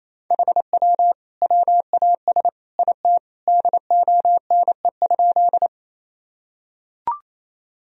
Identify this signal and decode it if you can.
Morse code, heard as a keyed tone, about 28 wpm, 700 Hz. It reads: HW WAS IT DONE?
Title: Sentences from Top 500 Words and Abbreviations 28wpm